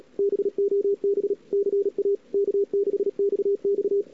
Morse code, heard as a keyed tone, about 38 wpm, 410 Hz. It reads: BOB CA K6XX
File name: CWSprint / Feb2007 / 2007-feb-cw-sprint-k5tr-op-n5rz -00:00